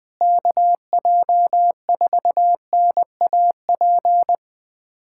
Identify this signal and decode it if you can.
Morse code, heard as a keyed tone, about 21 wpm, 700 Hz. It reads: KJ4NAP
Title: USA Callsign DE Callsign - 6 Characters - 20wpm